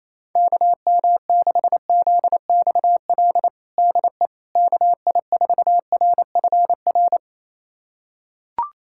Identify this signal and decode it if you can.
Morse code, heard as a keyed tone, about 28 wpm, 700 Hz. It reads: KM6ZXL DE KI4RFR
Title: USA Callsign DE Callsign 28wpm